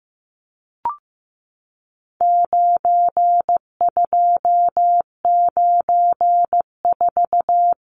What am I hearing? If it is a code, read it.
Morse code, heard as a keyed tone, about 15 wpm, 700 Hz. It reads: 9294
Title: Numbers - 4-Digit 15wpm